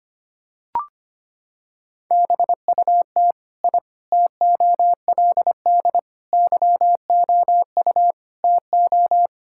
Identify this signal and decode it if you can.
Morse code, heard as a keyed tone, about 25 wpm, 700 Hz. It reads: BUT I TOLD YOU TO
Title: Sentences from Top 1000 Words - Review Part 6 - 25wpm